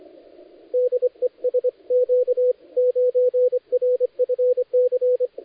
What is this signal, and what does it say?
Morse code, heard as a keyed tone, about 25 wpm, 500 Hz, with radio noise. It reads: DE SQ9RFC
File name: dlazaawansowanych / nagraniazpasma / titawaNETcontrol